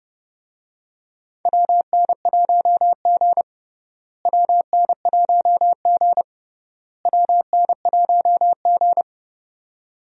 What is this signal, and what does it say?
Morse code, heard as a keyed tone, about 30 wpm, 700 Hz. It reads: WN1G WN1G WN1G
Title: USA Call Signs - 4 Characters - Repeated 3x - 30wpm